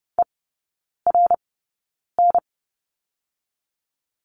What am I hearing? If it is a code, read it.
Morse code, heard as a keyed tone, about 30 wpm, spaced as for 9 wpm, 700 Hz. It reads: ERN
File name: Lesson 16e - Letter c - Three Characters 30wpm